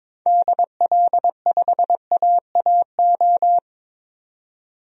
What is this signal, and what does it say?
Morse code, heard as a keyed tone, about 22 wpm, 700 Hz. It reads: DL5AAO
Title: DXCC-100 Call Signs - 6 Characters - 22wpm